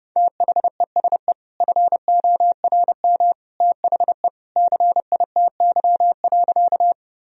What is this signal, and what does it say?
Morse code, heard as a keyed tone, about 30 wpm, 700 Hz. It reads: THESE FORM THE CITY.
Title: Sentences from Top 1000 Words - Review Part 1 - 30wpm